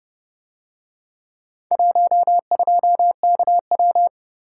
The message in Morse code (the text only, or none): 12KW